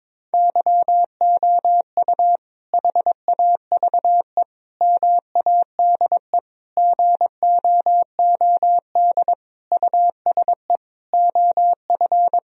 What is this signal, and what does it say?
Morse code, heard as a keyed tone, about 22 wpm, 700 Hz. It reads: YOU HAVE MADE GOOD USE OF
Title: Sentences from Top 1000 Words - Review Part 1 - 22wpm